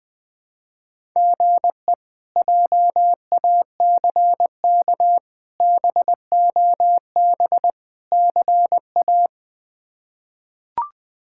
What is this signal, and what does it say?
Morse code, heard as a keyed tone, about 20 wpm, 700 Hz. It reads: GE JACK BOB CA
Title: K1USN SST QSO Practice - 20wpm